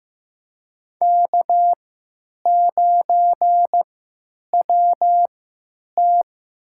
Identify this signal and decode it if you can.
Morse code, heard as a keyed tone, about 15 wpm, spaced as for 9 wpm, 700 Hz. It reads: K9WT
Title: USA Call Signs - 3x Character Spacing - 15wpm